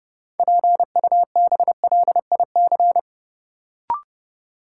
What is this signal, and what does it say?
Morse code, heard as a keyed tone, about 30 wpm, 700 Hz. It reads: PUBLIC
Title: Sets of 2 Words - New Word Drill - 30wpm